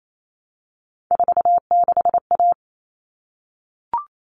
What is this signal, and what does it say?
Morse code, heard as a keyed tone, about 28 wpm, 700 Hz. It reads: 46A
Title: All Cut Numbers - 3-Digit 28wpm